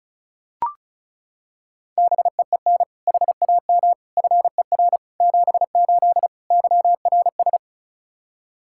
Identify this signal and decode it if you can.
Morse code, heard as a keyed tone, about 36 wpm, 700 Hz. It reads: BEEN HAM FER 78 YRS
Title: Practice QSO - Exchange 3 - More Info - 35wpm